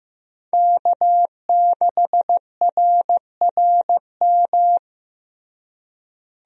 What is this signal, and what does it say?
Morse code, heard as a keyed tone, about 15 wpm, 700 Hz. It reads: K6RRM
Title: USA Call Signs - 5 Characters - Repeated 2x - 15wpm